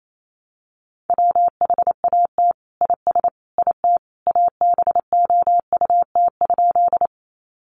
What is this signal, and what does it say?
Morse code, heard as a keyed tone, about 28 wpm, 700 Hz. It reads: WHAT IS IT ABOUT?
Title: Sentences from Top 100 Words 28wpm